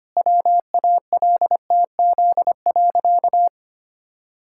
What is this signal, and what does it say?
Morse code, heard as a keyed tone, about 25 wpm, 700 Hz. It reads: WALTZ.
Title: Pangram Challenge v2 - 25wpm